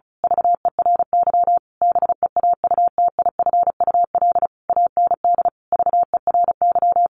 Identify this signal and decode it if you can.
Morse code, heard as a keyed tone, about 34 wpm, 700 Hz. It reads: VERY BEAUTIFUL AND VERY
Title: A Sound Of Thunder 35wpm